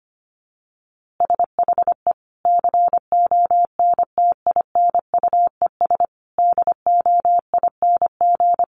Morse code, heard as a keyed tone, about 25 wpm, 700 Hz. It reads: SHE CONTINUES DOING
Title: Easy Phrases 25wpm